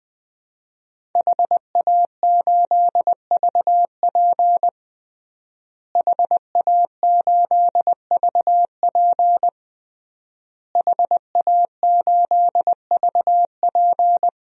Morse code, heard as a keyed tone, about 21 wpm, 700 Hz. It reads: HA8VP HA8VP HA8VP
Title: DXCC-100 Call Signs - 5 Characters - Repeated 3x - 20wpm